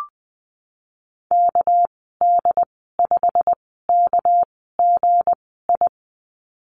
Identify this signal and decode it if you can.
Morse code, heard as a keyed tone, about 20 wpm, spaced as for 15 wpm, 700 Hz. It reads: KD5KGI
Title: 219 Lesson 35g - Number 0 - Call Signs 20wpm